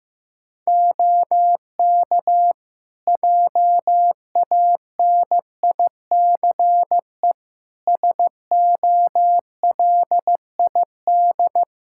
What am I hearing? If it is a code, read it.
Morse code, heard as a keyed tone, about 15 wpm, 700 Hz. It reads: OK JANICE SOLID
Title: K1USN SST QSO Practice 15wpm